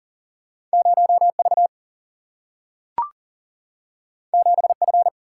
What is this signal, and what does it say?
Morse code, heard as a keyed tone, about 40 wpm, 700 Hz. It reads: 0 V 7 F
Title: Single-Letter Single-Number 40wpm